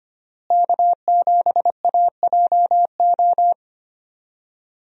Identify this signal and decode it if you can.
Morse code, heard as a keyed tone, about 25 wpm, 700 Hz. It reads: K7AJO
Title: USA Call Signs - 5 Characters - Repeated 3x - 25wpm